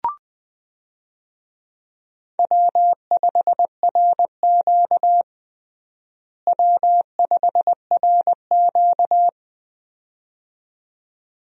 Morse code, heard as a keyed tone, about 21 wpm, 700 Hz. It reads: W5RQ W5RQ
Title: USA Call Signs - 4 Characters - Repeated 2x - 20wpm